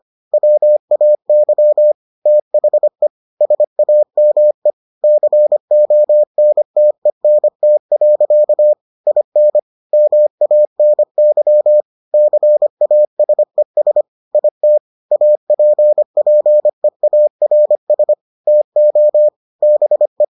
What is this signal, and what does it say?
Morse code, heard as a keyed tone, about 25 wpm, 600 Hz. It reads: WAY THE SAME CONTENT. IN MANY CASES IT APPEARS TO BE